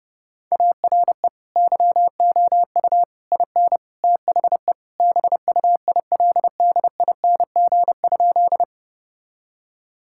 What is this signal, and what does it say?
Morse code, heard as a keyed tone, about 32 wpm, 700 Hz. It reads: ARE YOU IN THE BUILDING?
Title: Sentences from Top 1000 Words - Review Part 5 - 30wpm